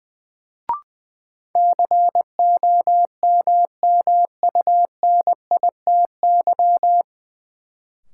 Morse code, heard as a keyed tone, about 20 wpm, 700 Hz. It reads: COMMUNITY
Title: Sentences from Top 1000 Words - Review Part 4 - New Word Drill - 20wpm